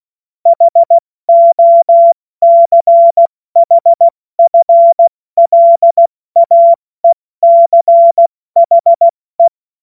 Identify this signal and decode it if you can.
Morse code, heard as a keyed tone, about 16 wpm, spaced as for 14 wpm, 680 Hz. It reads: HOCHFLAECHE
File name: Morsekurs / Witze / Witz 16. August 70 F80